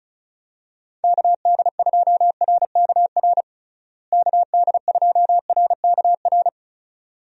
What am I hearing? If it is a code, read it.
Morse code, heard as a keyed tone, about 34 wpm, 700 Hz. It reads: KD2RKR KD2RKR
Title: USA Call Signs - 6 Characters - Repeated 2x - 35wpm